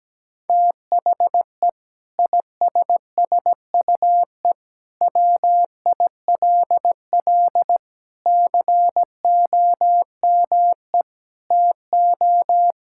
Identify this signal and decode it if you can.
Morse code, heard as a keyed tone, about 17 wpm, 700 Hz. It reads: THE ISSUE WILL COME TO
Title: Sentences from Top 1000 Words - Review Part 6 - 17wpm